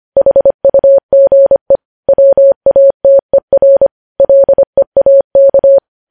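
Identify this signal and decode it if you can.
Morse code, heard as a keyed tone, about 25 wpm, 550 Hz. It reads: HUGE WATER LEAK
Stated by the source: Morse code news headlines sent at 25 words per minute.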